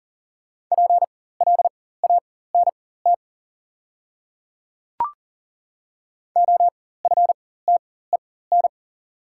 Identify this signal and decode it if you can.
Morse code, heard as a keyed tone, about 40 wpm, spaced as for 19 wpm, 700 Hz. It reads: PLANT OFTEN
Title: 4x Character Spacing - Top 300 Words - 40wpm